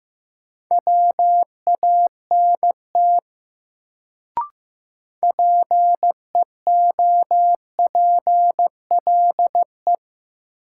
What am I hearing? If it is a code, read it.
Morse code, heard as a keyed tone, about 15 wpm, 700 Hz. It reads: WANT PEOPLE
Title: Sentences from Top 200 Words - New Word Drill - 15wpm